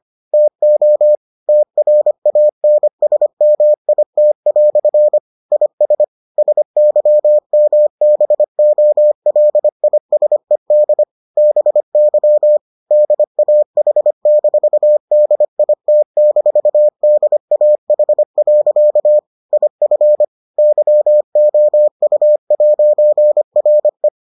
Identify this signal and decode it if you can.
Morse code, heard as a keyed tone, about 25 wpm, 600 Hz. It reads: TO TRANSMIT" IS SYMBOLISED BY DAH-DIT-DAH. IF YOU'RE